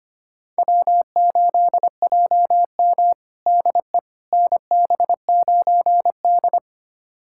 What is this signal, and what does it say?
Morse code, heard as a keyed tone, about 25 wpm, 700 Hz. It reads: W8JM DE NB9D
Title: USA Callsign DE Callsign - 4 Characters - 25wpm